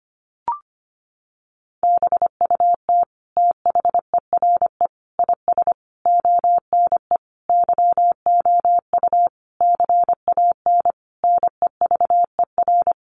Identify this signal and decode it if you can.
Morse code, heard as a keyed tone, about 25 wpm, 700 Hz. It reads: BUT THERE IS ONE YOU CAN NEVER
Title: Sentences from Top 1200 Words 25wpm